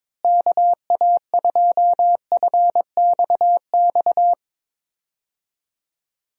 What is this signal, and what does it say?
Morse code, heard as a keyed tone, about 22 wpm, 700 Hz. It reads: KA2FXX
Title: USA Call Signs - 6 Characters - Repeated 3x - 22wpm